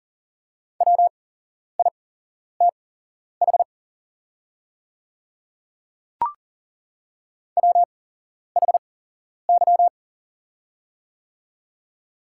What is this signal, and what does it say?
Morse code, heard as a keyed tone, about 42 wpm, spaced as for 11 wpm, 700 Hz. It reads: WITH WHY
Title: 8x Character Spacing - Top 200 Words - 40wpm